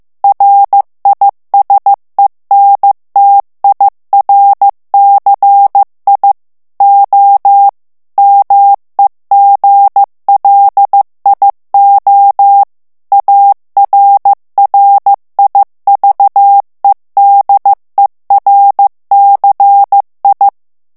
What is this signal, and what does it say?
Morse code, heard as a keyed tone, about 15 wpm, 800 Hz. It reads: RISENTIRCI O MEGLIO ARRIVEDERCI
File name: ITALIEN texte clair 1 vitesse 14 mots minute